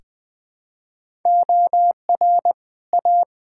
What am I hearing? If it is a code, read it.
Morse code, heard as a keyed tone, about 20 wpm, 700 Hz. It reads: OR A
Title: Build A Phrase - 2-Word Phrases - 20wpm